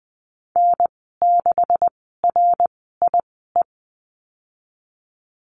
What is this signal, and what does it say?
Morse code, heard as a keyed tone, about 20 wpm, spaced as for 15 wpm, 700 Hz. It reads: N6RIE
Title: 154 Lesson 25g - Letter p - Call Signs 20wpm